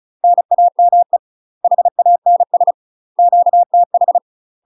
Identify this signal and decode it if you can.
Morse code, heard as a keyed tone, about 36 wpm, 700 Hz. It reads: NAME HANS QTH
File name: UK MASTER QSOs TRAINING / 35WPM / UK MASTER QSOs 0023